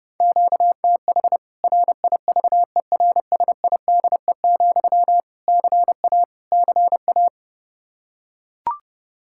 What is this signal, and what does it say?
Morse code, heard as a keyed tone, about 30 wpm, 700 Hz. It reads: QTH RIVERSIDE, CA CA
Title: K1USN SST QSO Practice 30wpm